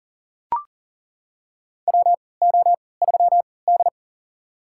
Morse code, heard as a keyed tone, about 37 wpm, spaced as for 22 wpm, 700 Hz. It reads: WO3D
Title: USA Call Signs - 4 Characters - 3x Character Spacing - 40wpm